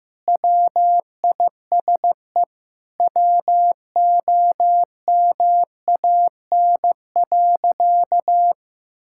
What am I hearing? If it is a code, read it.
Morse code, heard as a keyed tone, about 15 wpm, 700 Hz. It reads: WISE WOMAN.
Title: Sentences from Top 1600 Words 15wpm